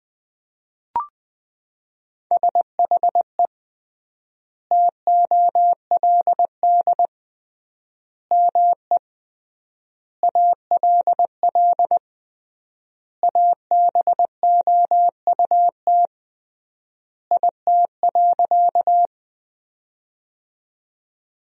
Morse code, heard as a keyed tone, about 20 wpm, 700 Hz. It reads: SHE TOLD ME ALL ABOUT IT.
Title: 3x Word-Spacing - Sentences from Top 400 Words - 20wpm